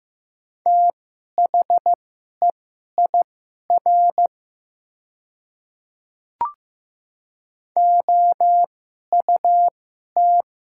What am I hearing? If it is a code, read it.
Morse code, heard as a keyed tone, about 15 wpm, spaced as for 11 wpm, 700 Hz. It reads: THEIR OUT
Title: 2x Character Spacing - Top 200 Words - 15wpm